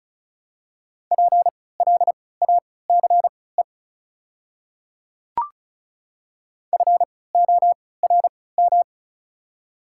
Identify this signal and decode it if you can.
Morse code, heard as a keyed tone, about 36 wpm, spaced as for 20 wpm, 700 Hz. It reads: PLACE FORM
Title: 3x Character Spacing - Top 200 Words - 35wpm